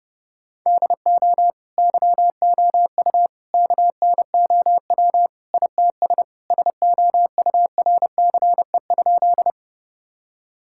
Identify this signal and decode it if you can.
Morse code, heard as a keyed tone, about 32 wpm, 700 Hz. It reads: DO YOU KNOW ITS SOURCE?
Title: Sentences from Top 1000 Words - Review Part 5 - 30wpm